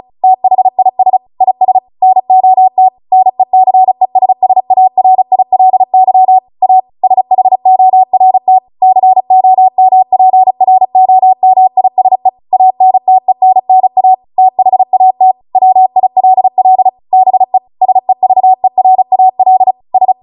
Morse code, heard as a keyed tone, about 36 wpm, 750 Hz. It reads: THIS IS NOT NECESSARILY A SHORT COMPROMISE ANTENNA THAT WILL BE SEVERAL S
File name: Morse / 35WPM